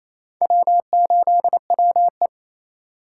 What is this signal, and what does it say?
Morse code, heard as a keyed tone, about 33 wpm, 700 Hz. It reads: W8WE
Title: Speed Racing - USA 1x2 Call Signs 28wpm